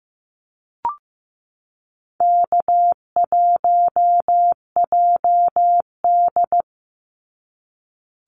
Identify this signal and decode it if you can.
Morse code, heard as a keyed tone, about 15 wpm, 700 Hz. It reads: K1JD
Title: NA SOTA Activator Callsigns 15wpm